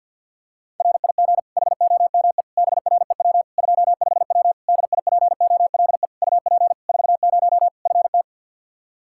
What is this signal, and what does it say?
Morse code, heard as a keyed tone, about 52 wpm, 700 Hz. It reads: RIG HOME BREW 25W DIPOLE UP 40 FT
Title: K1USN SST QSO Practice 50wpm